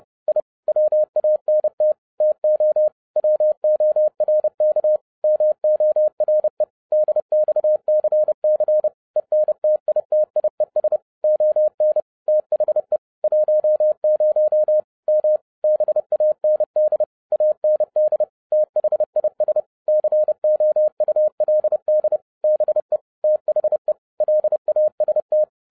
Morse code, heard as a keyed tone, about 30 wpm, 610 Hz. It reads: I WANT TO WORK MORE DXCC ENTITIES ON THE 10 M BAND AND THIS COULD BE THE LAST
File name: 30wpm